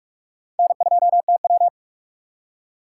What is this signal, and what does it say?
Morse code, heard as a keyed tone, about 66 wpm, spaced as for 54 wpm, 700 Hz. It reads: N2TW
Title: Speed Racing - USA 1x2 Call Signs 45wpm